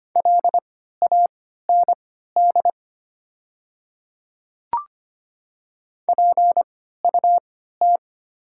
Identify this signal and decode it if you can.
Morse code, heard as a keyed tone, about 25 wpm, spaced as for 14 wpm, 700 Hz. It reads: LAND PUT
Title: 3x Character Spacing - Top 300 Words - 25wpm